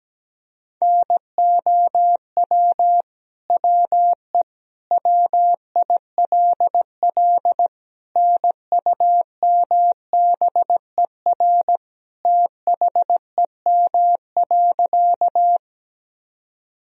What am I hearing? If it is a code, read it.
Morse code, heard as a keyed tone, about 17 wpm, 700 Hz. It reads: NOW WE WILL NUMBER THEM.
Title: Sentences from Top 1000 Words - Review Part 1 - 17wpm